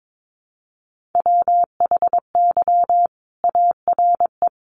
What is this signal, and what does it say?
Morse code, heard as a keyed tone, about 22 wpm, 700 Hz. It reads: WHY ARE
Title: Sentences from Top 800 Words 22wpm